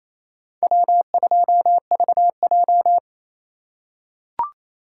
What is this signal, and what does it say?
Morse code, heard as a keyed tone, about 27 wpm, 700 Hz. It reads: W2VJ
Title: USA Call Signs - 4 Characters - 2x Character Spacing - 28wpm